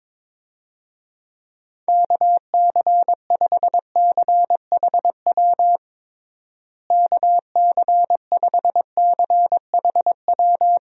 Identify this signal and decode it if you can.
Morse code, heard as a keyed tone, about 22 wpm, 700 Hz. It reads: KC5CHW KC5CHW
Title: USA Call Signs - 6 Characters - Repeated 3x - 22wpm